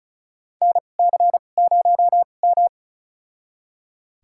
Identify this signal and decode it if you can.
Morse code, heard as a keyed tone, about 36 wpm, spaced as for 25 wpm, 700 Hz. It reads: NC0M
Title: USA Call Signs - 4 Characters - 2x Character Spacing - 35wpm